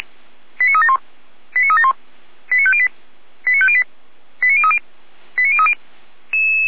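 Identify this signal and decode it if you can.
Five-tone selective call (ZVEI), heard as quick runs of five tones, tones of about 70 ms. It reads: 89371 89371 89508 89508 89030 89030
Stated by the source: ZVEI five-tone selective call received with an SDR, archived by the Orlamünde fire brigade (feuerwehr-orlamuende).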